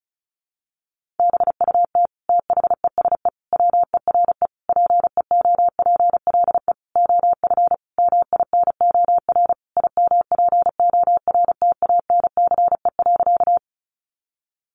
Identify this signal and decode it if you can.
Morse code, heard as a keyed tone, about 34 wpm, 700 Hz. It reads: BUT THESE WERE PEOPLE OF MINOR IMPORTANCE.
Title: Sentences from Top 1500 Words 35wpm